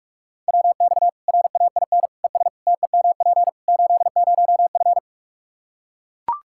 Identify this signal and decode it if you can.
Morse code, heard as a keyed tone, about 43 wpm, 700 Hz. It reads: WX RAIN ES TEMP 80F
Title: Practice QSO - Exchange 2 - Info - 45wpm